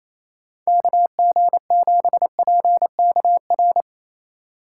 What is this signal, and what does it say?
Morse code, heard as a keyed tone, about 28 wpm, 700 Hz. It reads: KG7PKR
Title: USA Call Signs - 6 Characters - 3x Character Spacing - 28wpm